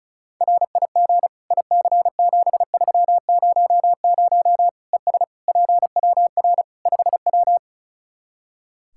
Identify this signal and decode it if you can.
Morse code, heard as a keyed tone, about 36 wpm, 700 Hz. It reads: RIG IC7300 ES PWR 5W
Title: Practice QSO - Exchange 2 - Info - 35wpm